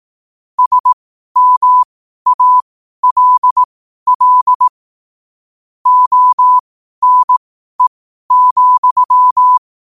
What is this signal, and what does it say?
Morse code, heard as a keyed tone, about 18 wpm, spaced as for 12 wpm, 1000 Hz. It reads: SMALL ONE,
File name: qotdfiles / 12wpm